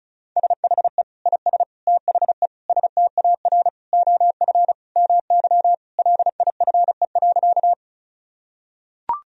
Morse code, heard as a keyed tone, about 36 wpm, 700 Hz. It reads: SHE IS THE STAR OF MY LIFE.
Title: Sentences from Top 1000 Words - Review Part 2 - 35wpm